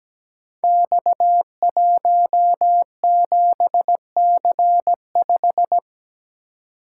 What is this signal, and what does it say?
Morse code, heard as a keyed tone, about 17 wpm, 700 Hz. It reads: X17C5
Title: Code Groups - 5 Letter:Numbers - 17wpm